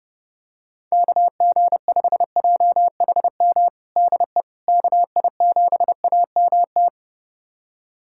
Morse code, heard as a keyed tone, about 32 wpm, 700 Hz. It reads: KG5JHM DE KI7AMT
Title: USA Callsign DE Callsign - 6 Characters - 30wpm